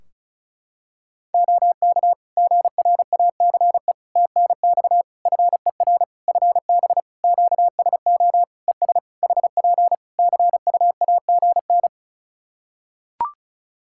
Morse code, heard as a keyed tone, about 36 wpm, 700 Hz. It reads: OK GRACE TNX FER FB QSO ES HP CUAGN
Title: Practice QSO - Exchange 4 - Ending - 35wpm